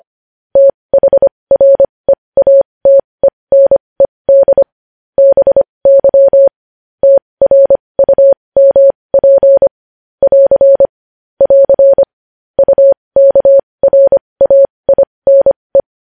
Morse code, heard as a keyed tone, about 25 wpm, spaced as for 20 wpm, 550 Hz. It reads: THREATENED BY TRUMP <AR> <AR> UKRAINE
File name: podcast / News headlines-1x-25x20